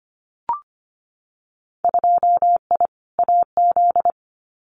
Morse code, heard as a keyed tone, about 25 wpm, 700 Hz. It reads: 2I AZ
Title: Winter Field Day Exchanges 25wpm